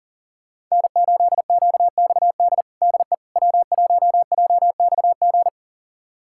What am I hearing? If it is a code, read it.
Morse code, heard as a keyed tone, about 40 wpm, 700 Hz. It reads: N8QXD DE W1JXG
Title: USA Callsign DE Callsign 40wpm